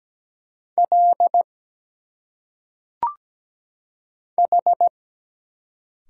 Morse code, heard as a keyed tone, about 17 wpm, 700 Hz. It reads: L H
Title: Single Letters - Phonetic Pronunciation 17wpm